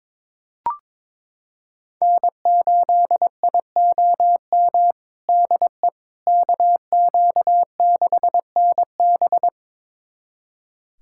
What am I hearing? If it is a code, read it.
Morse code, heard as a keyed tone, about 23 wpm, 700 Hz. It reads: N8IOM DE KQ6NB
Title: USA Callsign DE Callsign - 5 Characters - 22wpm